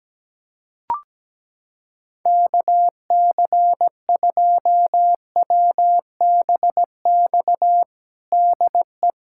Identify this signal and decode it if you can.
Morse code, heard as a keyed tone, about 17 wpm, 700 Hz. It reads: KC2WBX DE
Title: USA Callsign DE Callsign - 6 Characters - 17wpm